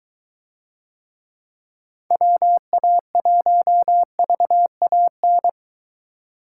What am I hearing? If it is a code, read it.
Morse code, heard as a keyed tone, about 23 wpm, 700 Hz. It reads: WA1VAN